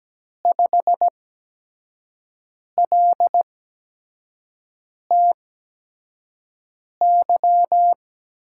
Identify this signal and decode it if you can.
Morse code, heard as a keyed tone, about 17 wpm, spaced as for 5 wpm, 700 Hz. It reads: 5LTY
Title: USA Call Signs - 6 Characters - 8x Character Spacing - 17wpm